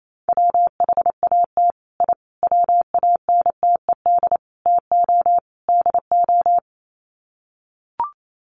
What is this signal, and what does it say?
Morse code, heard as a keyed tone, about 28 wpm, 700 Hz. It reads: WHAT I WANTED TO DO
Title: Sets of 5 Words Encore2 28wpm